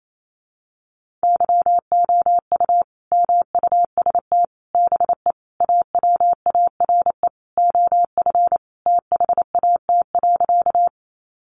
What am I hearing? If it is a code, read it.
Morse code, heard as a keyed tone, about 28 wpm, 700 Hz. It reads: YOU MUST BE AWARE OF THAT.
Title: Sentences from Top 1200 Words 28wpm